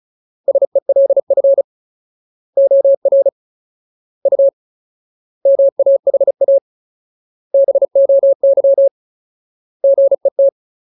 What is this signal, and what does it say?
Morse code, heard as a keyed tone, about 34 wpm, 550 Hz. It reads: SELF OR U MAHA BOY GET
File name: Ultimate Word List / 35WPM Ultimate